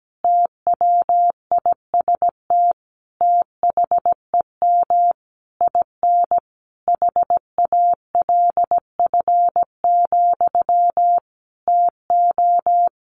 Morse code, heard as a keyed tone, about 17 wpm, 700 Hz. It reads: TWIST THEM IN HALF, TO
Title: A Sound Of Thunder 17wpm